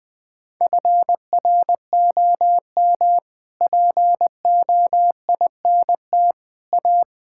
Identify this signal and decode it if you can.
Morse code, heard as a keyed tone, about 20 wpm, 700 Hz. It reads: FROM POINT A
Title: Sets of 6 Words - 20wpm